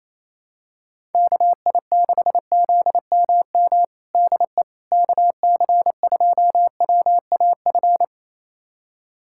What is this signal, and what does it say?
Morse code, heard as a keyed tone, about 28 wpm, 700 Hz. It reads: KI6ZMM DE KC2WAF
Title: USA Callsign DE Callsign 28wpm